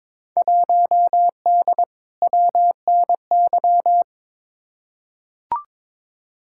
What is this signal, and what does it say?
Morse code, heard as a keyed tone, about 22 wpm, 700 Hz. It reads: 1D WNY
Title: Field Day Exchanges - 22wpm